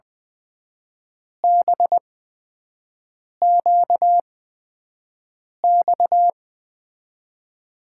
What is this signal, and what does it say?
Morse code, heard as a keyed tone, about 20 wpm, spaced as for 5 wpm, 700 Hz. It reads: BQX
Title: USA Call Signs - 6 Characters - 8x Character Spacing - 20wpm